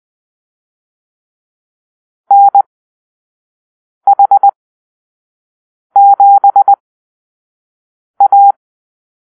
Morse code, heard as a keyed tone, about 20 wpm, spaced as for 5 wpm, 800 Hz. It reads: NH7A
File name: Call-Sign Drill